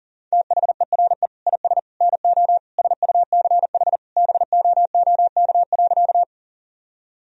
Morse code, heard as a keyed tone, about 40 wpm, 700 Hz. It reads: THERE IS NO SUCH BOOK.
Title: Sentences from Top 1000 Words - Review Part 1 - 40wpm